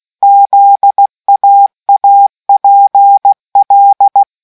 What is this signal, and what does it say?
Morse code, heard as a keyed tone, about 16 wpm, 800 Hz. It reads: ZAAPL